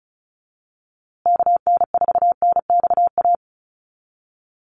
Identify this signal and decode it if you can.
Morse code, heard as a keyed tone, about 34 wpm, 700 Hz. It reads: KN4NXA
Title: USA Call Signs Repeated 3x 35wpm